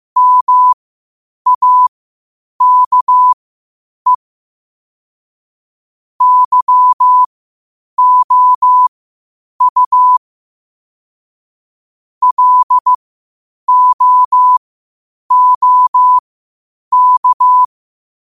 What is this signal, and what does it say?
Morse code, heard as a keyed tone, about 15 wpm, spaced as for 8 wpm, 1000 Hz. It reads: MAKE YOU LOOK
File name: qotdfiles / 8wpm